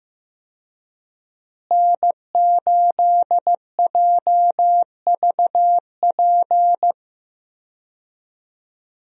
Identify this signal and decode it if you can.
Morse code, heard as a keyed tone, about 15 wpm, 700 Hz. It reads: N8JVP
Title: USA Call Signs - 5 Characters - Repeated 2x - 15wpm